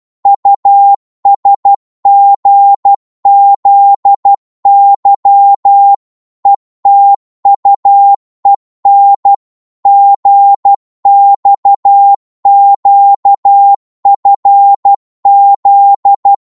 Morse code, heard as a keyed tone, about 12 wpm, 800 Hz. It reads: USGZY ETUEN GXQFZ
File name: Morse / ltr-12wpm